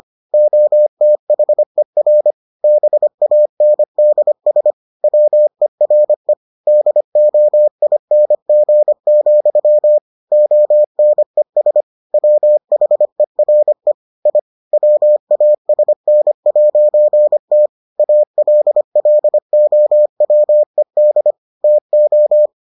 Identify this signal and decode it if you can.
Morse code, heard as a keyed tone, about 25 wpm, 600 Hz. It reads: OTHER BANDS WERE DOING, ONES WHERE I WASN'T ALLOWED TO